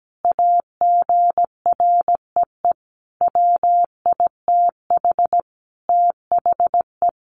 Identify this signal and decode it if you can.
Morse code, heard as a keyed tone, about 17 wpm, 700 Hz. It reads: AGREE WITH THE
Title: Sentences from Top 1400 Words 17wpm